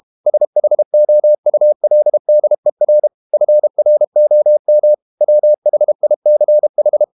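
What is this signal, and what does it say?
Morse code, heard as a keyed tone, about 32 wpm, 600 Hz. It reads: SHOULDER FROM WHICH
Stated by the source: Morse code practice at 32 wpm from CW med Gnister, reading Ken Follett's novel Never in English.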